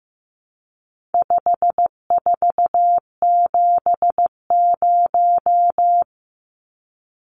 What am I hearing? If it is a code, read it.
Morse code, heard as a keyed tone, about 15 wpm, 700 Hz. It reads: 5470
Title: Numbers - 4-Digit 15wpm